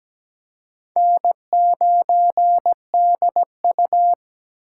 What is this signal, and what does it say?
Morse code, heard as a keyed tone, about 17 wpm, 700 Hz. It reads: N9DU
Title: USA Call Signs - 4 Characters - 3x Character Spacing - 17wpm